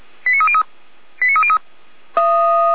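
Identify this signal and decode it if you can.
Five-tone selective call (ZVEI), heard as quick runs of five tones, tones of about 70 ms. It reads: 89303 89303
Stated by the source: Received by SDR on a ZVEI selective-call channel, archived by Feuerwehr Orlamünde.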